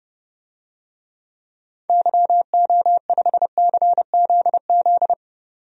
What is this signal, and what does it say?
Morse code, heard as a keyed tone, about 30 wpm, 700 Hz. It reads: YO5CZZ
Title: DXCC-100 Call Signs - 6 Characters - 30wpm